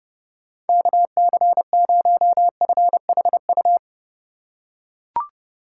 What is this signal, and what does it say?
Morse code, heard as a keyed tone, about 30 wpm, 700 Hz. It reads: KC0FHU
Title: USA Call Signs - 6 Characters - 4x Character Spacing - 30wpm